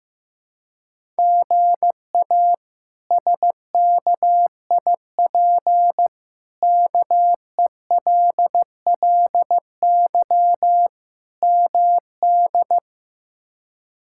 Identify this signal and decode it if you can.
Morse code, heard as a keyed tone, about 15 wpm, 700 Hz. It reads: GA SKIP KELLY MD
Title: K1USN SST QSO Practice - 15wpm